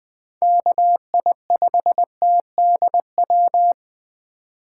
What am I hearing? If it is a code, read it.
Morse code, heard as a keyed tone, about 20 wpm, 700 Hz. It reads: KI5TDW
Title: 2023 POTA Activator Call Signs - 20wpm